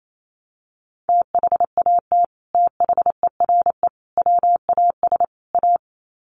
Morse code, heard as a keyed tone, about 28 wpm, 700 Hz. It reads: THAT THERE WAS A
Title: Sets of 4 Words 28wpm